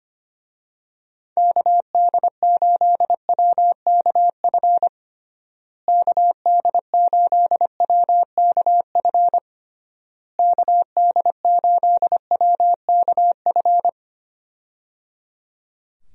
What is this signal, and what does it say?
Morse code, heard as a keyed tone, about 25 wpm, 700 Hz. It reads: KD8WKF KD8WKF KD8WKF
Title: USA Call Signs - 6 Characters - Repeated 3x - 25wpm